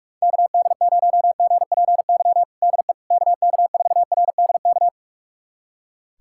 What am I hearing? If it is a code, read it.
Morse code, heard as a keyed tone, about 46 wpm, 700 Hz. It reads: KD0GPY DE KK4RDK
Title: USA Callsign DE Callsign 45wpm